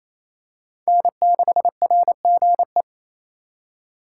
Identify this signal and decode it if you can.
Morse code, heard as a keyed tone, about 28 wpm, 700 Hz. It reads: N6RGE
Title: USA Call Signs - 4x Character Spacing - 28wpm